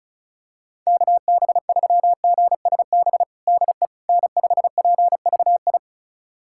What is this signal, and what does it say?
Morse code, heard as a keyed tone, about 36 wpm, 700 Hz. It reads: KB3GSB DE N5PVI
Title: USA Callsign DE Callsign 35wpm